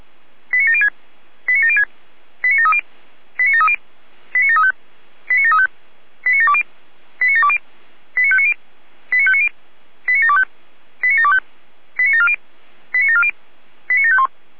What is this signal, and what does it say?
Five-tone selective call (ZVEI), heard as quick runs of five tones, tones of about 70 ms. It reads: 89706 89706 89730 89730 89735 89735 89720 89720 89590 89590 89725 89725 89740 89740 89751